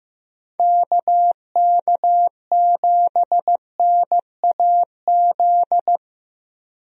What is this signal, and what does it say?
Morse code, heard as a keyed tone, about 15 wpm, 700 Hz. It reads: KK7NAZ
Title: USA Call Signs - 6 Characters - Repeated 3x - 15wpm